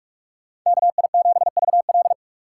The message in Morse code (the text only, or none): KI7VL